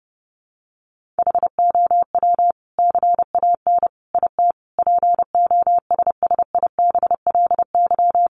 Morse code, heard as a keyed tone, about 28 wpm, 700 Hz. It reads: HOW CAN IT POSSIBLY
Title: Sentences from Top 1300 Words 30wpm